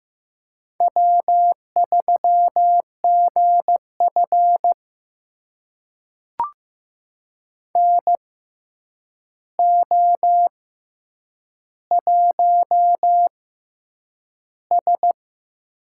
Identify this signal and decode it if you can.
Morse code, heard as a keyed tone, about 15 wpm, spaced as for 8 wpm, 700 Hz. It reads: W3GF NO1S
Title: USA Call Signs - 6x Character Spacing - 15wpm